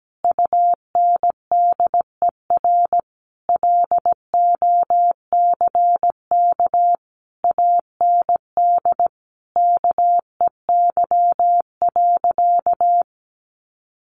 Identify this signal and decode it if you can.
Morse code, heard as a keyed tone, about 17 wpm, 700 Hz. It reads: UNDER LOCK AND KEY.
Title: Sentences from Top 1400 Words 17wpm